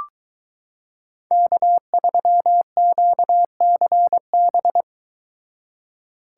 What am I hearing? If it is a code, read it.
Morse code, heard as a keyed tone, about 22 wpm, 700 Hz. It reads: K3QCB